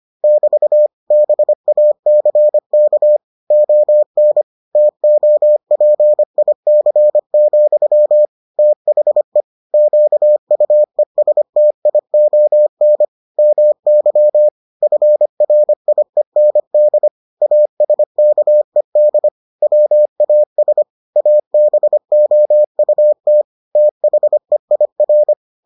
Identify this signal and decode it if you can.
Morse code, heard as a keyed tone, about 25 wpm, 600 Hz. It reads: <BT> BACK ON TOPIC, THE QUESTION MY FRIEND ASKED WAS ABOUT THEIR